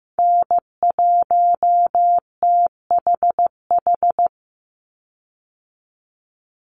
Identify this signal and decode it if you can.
Morse code, heard as a keyed tone, about 15 wpm, 700 Hz. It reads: N1THH
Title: USA Call Signs 15wpm